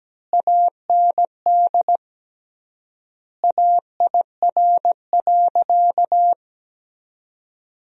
Morse code, heard as a keyed tone, about 17 wpm, 700 Hz. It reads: AND AIR.
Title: 3x Word-Spacing - Sentences from Top 400 Words - 17wpm